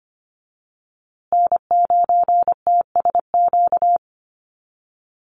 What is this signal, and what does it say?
Morse code, heard as a keyed tone, about 25 wpm, 700 Hz. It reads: N9TSQ
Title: USA Call Signs Repeated 2x 25wpm